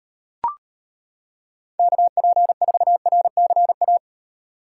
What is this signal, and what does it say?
Morse code, heard as a keyed tone, about 31 wpm, 700 Hz. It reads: KP4RCA
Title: Speed Racing - USA 2x3 Call Signs 25wpm